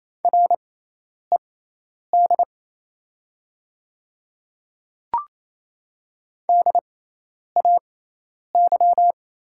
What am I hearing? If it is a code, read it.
Morse code, heard as a keyed tone, about 28 wpm, spaced as for 10 wpm, 700 Hz. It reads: RED DAY
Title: 6x Character Spacing - Top 500 Words - 28wpm